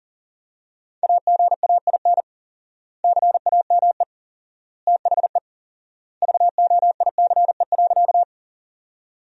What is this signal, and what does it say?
Morse code, heard as a keyed tone, about 40 wpm, 700 Hz. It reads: AGAIN CAME THE VOICE.
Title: 4x Word-Spacing - Sentences from Top 500 Words - 40wpm